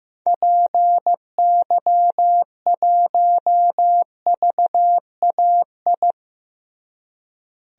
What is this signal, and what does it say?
Morse code, heard as a keyed tone, about 15 wpm, 700 Hz. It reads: PY1VAI
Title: DXCC-100 Call Signs - 6 Characters - Repeated 2x - 15wpm